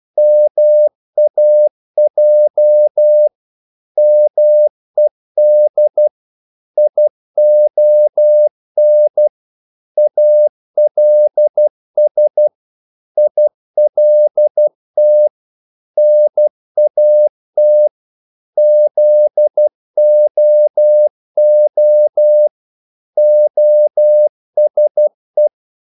Morse code, heard as a keyed tone, about 12 wpm, 600 Hz. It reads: MAJ MED ION ALS ILT NAT ZOO OSE
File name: Korte ord DK 12wpm